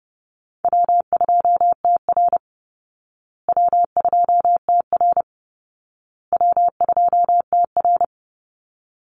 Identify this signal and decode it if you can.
Morse code, heard as a keyed tone, about 30 wpm, 700 Hz. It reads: W2TR W2TR W2TR
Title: Easy Call Signs Repeated 3x - 30wpm